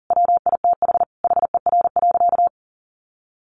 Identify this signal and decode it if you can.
Morse code, heard as a keyed tone, about 40 wpm, 700 Hz. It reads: WITH HER.
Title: Sentences from Top 1500 Words 40wpm